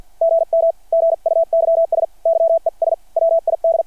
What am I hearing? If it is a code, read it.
Morse code, heard as a keyed tone, about 50 wpm, spaced as for 44 wpm, 660 Hz. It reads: GM GUYS YES WID
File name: archived / CRSnet